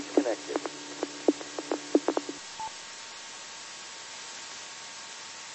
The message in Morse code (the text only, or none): none